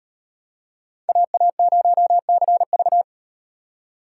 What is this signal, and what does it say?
Morse code, heard as a keyed tone, about 38 wpm, 700 Hz. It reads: AA0CV